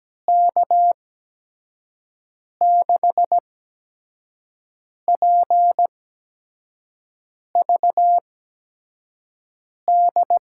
Morse code, heard as a keyed tone, about 17 wpm, spaced as for 5 wpm, 700 Hz. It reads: K6PVD
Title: USA Call Signs - 6 Characters - 8x Character Spacing - 17wpm